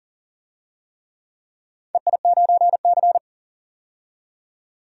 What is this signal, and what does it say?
Morse code, heard as a keyed tone, about 40 wpm, 700 Hz. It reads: EI9C
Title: DXCC-100 Call Signs - 4 Characters - 40wpm